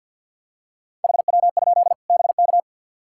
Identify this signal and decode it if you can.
Morse code, heard as a keyed tone, about 51 wpm, 700 Hz. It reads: HW? BK
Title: K1USN SST QSO Practice 50wpm